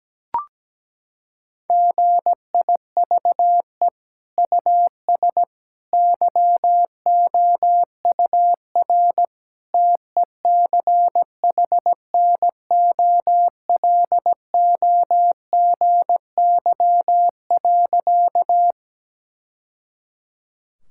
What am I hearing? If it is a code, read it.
Morse code, heard as a keyed tone, about 17 wpm, 700 Hz. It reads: GIVE US YOUR TECHNOLOGY.
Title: Sentences from Top 1000 Words - Review Part 5 - 17wpm